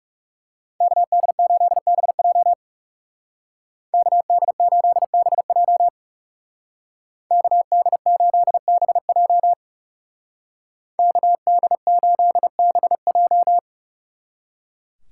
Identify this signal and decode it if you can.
Morse code, heard as a keyed tone, about 38 wpm, 700 Hz. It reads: KD8BJ KD8BJ KD8BJ KD8BJ